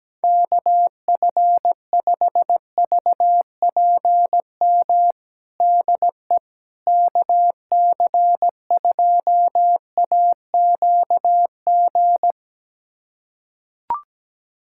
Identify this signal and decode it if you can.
Morse code, heard as a keyed tone, about 17 wpm, 700 Hz. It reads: KF5VPM DE KC2AQG
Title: USA Callsign DE Callsign - 6 Characters - 17wpm